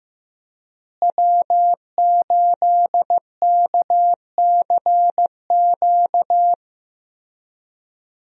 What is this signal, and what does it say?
Morse code, heard as a keyed tone, about 16 wpm, 700 Hz. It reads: W8KCQ
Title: USA Call Signs - 5 Characters - Repeated 2x - 15wpm